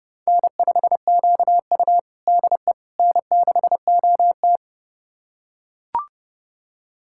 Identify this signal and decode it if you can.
Morse code, heard as a keyed tone, about 29 wpm, 700 Hz. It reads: N5QU DE N6OT
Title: USA Callsign DE Callsign - 4 Characters - 30wpm